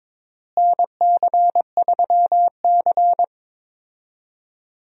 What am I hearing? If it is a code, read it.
Morse code, heard as a keyed tone, about 22 wpm, 700 Hz. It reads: NC3C
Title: USA Call Signs - 4 Characters - Repeated 3x - 22wpm